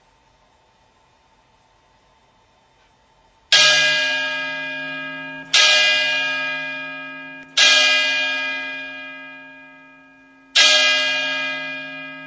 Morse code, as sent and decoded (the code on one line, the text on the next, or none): none